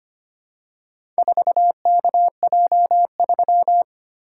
Morse code, heard as a keyed tone, about 25 wpm, 700 Hz. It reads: 4KJ3
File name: practice / 25wpm / Four Letter-Number Combinations - 25wpm